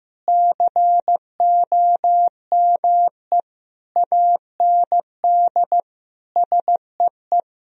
Morse code, heard as a keyed tone, about 15 wpm, 700 Hz. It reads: COME AND SEE
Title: Sentences from Top 1000 Words - Review Part 4 - 15wpm